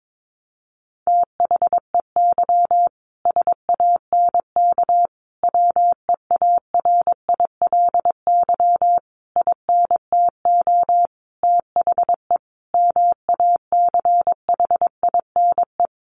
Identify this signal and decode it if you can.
Morse code, heard as a keyed tone, about 22 wpm, 700 Hz. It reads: THEY SANK WEARILY INTO THE MACHINE
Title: A Sound Of Thunder 22wpm